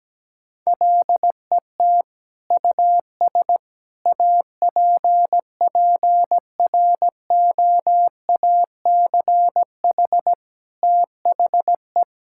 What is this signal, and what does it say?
Morse code, heard as a keyed tone, about 17 wpm, 700 Hz. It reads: LET US APPROACH THE
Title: Sentences from Top 1000 Words - Review Part 6 - 17wpm